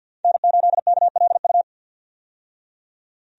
Morse code, heard as a keyed tone, about 51 wpm, 700 Hz. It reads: N8KLU
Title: USA Call Signs - 5 Characters - 50wpm